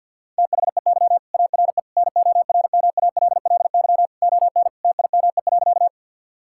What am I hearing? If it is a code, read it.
Morse code, heard as a keyed tone, about 50 wpm, 700 Hz. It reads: THEY ARE NORMALLY ON TIME.
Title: Sentences from Top 1000 Words - Review Part 6 - 50wpm